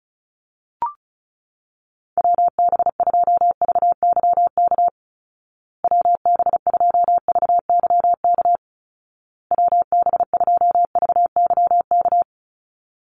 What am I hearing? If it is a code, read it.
Morse code, heard as a keyed tone, about 34 wpm, 700 Hz. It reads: WB2VYK WB2VYK WB2VYK
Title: Call Signs Repeated 2x - 35wpm